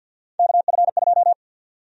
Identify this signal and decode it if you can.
Morse code, heard as a keyed tone, about 46 wpm, 700 Hz. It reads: KF2
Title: USA Call Sign Prefixes - Repeated 2x - 50wpm